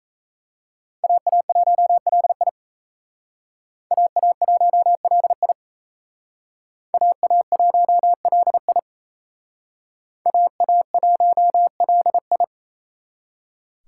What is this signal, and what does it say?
Morse code, heard as a keyed tone, about 35 wpm, 700 Hz. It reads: AA1LI AA1LI AA1LI AA1LI